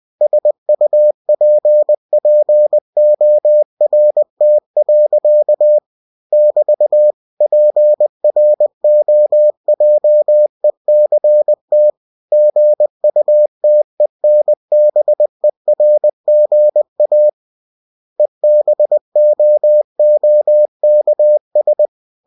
Morse code, heard as a keyed tone, about 20 wpm, 600 Hz. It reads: SUPPORT. <BT> PROJECT GUTENBERGA EBOOKS
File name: cw-books / winnie-the-pooh / winnie-20wpm-page-0122